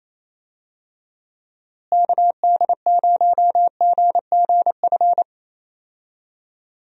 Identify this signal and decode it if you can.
Morse code, heard as a keyed tone, about 28 wpm, 700 Hz. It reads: KD0GGF
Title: USA Call Signs - 6 Characters - Repeated 2x - 28wpm